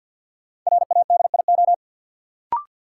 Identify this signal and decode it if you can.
Morse code, heard as a keyed tone, about 49 wpm, 700 Hz. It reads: RADIO
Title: Sentences from Top 1000 Words - Review Part 3 - New Word Drill - 50wpm